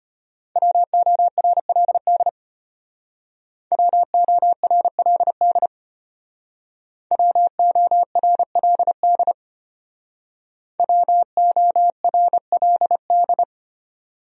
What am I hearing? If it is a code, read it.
Morse code, heard as a keyed tone, about 30 wpm, 700 Hz. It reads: WORLD WORLD WORLD WORLD
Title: Speed-Racing Top 200 Words 25wpm